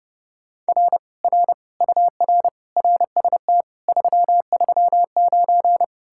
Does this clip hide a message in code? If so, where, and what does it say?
Yes: Morse code from 0.7 to 5.8 s, heard as a keyed tone, about 40 wpm, 700 Hz. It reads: R R UR RST 339